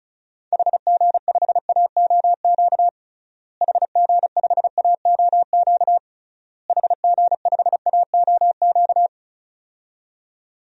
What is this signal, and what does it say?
Morse code, heard as a keyed tone, about 36 wpm, 700 Hz. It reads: HG5AOQ HG5AOQ HG5AOQ
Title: DXCC-100 Call Signs - 6 Characters - Repeated 3x - 35wpm